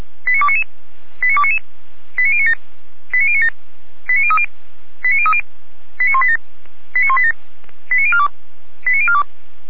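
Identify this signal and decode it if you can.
Five-tone selective call (ZVEI), heard as quick runs of five tones, tones of about 70 ms. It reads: 89299 89299 89097 89097 89030 89030 89187 89187 89042 89042
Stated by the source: ZVEI five-tone selective call 89294 received on SDR.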